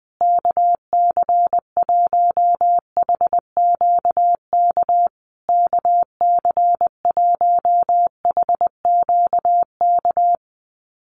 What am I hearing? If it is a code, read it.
Morse code, heard as a keyed tone, about 20 wpm, 700 Hz. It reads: KC1HQK KC1HQK
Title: USA Call Signs Repeated 3x 20wpm